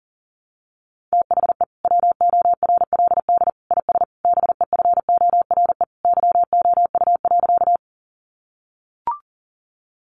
Morse code, heard as a keyed tone, about 40 wpm, 700 Hz. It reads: THE WORLD IS BEFORE YOU.
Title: Sentences from Top 200 Words 40wpm